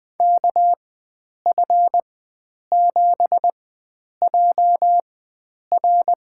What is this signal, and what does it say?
Morse code, heard as a keyed tone, about 20 wpm, spaced as for 9 wpm, 700 Hz. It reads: KF7JR
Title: USA Call Signs - 5 Characters - 4x Character Spacing - 20wpm